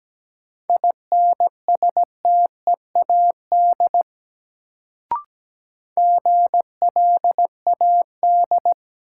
Morse code, heard as a keyed tone, about 17 wpm, 700 Hz. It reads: INSTEAD GLAD
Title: Sentences from Top 800 Words - New Word Drill - 17wpm